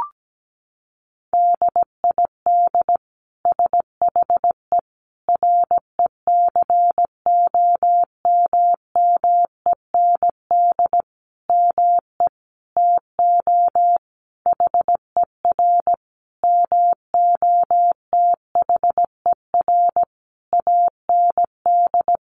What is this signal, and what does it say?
Morse code, heard as a keyed tone, about 17 wpm, 700 Hz. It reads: DID SHE RECOMMEND ME TO HER MOTHER AND
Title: Sentences from Top 1200 Words 17wpm